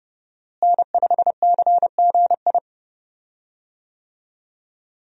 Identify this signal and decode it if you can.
Morse code, heard as a keyed tone, about 30 wpm, 700 Hz. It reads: N5CGI
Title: USA Call Signs - 5 Characters - Repeated 2x - 30wpm